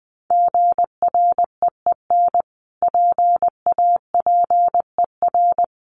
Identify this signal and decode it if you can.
Morse code, heard as a keyed tone, about 20 wpm, 700 Hz. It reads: GREEN PAPER
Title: Sentences from Top 1500 Words 20wpm